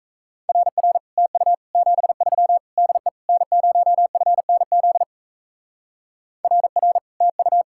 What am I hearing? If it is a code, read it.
Morse code, heard as a keyed tone, about 32 wpm, 700 Hz. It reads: RR TU 73 DE N0FNZ RR TU
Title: Speed-Racing Common QSO Phrases - 28wpm